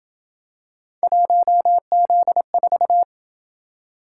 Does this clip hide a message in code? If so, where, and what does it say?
Yes: Morse code from 1.0 to 3.0 s, heard as a keyed tone, about 27 wpm, 700 Hz. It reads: 1Z4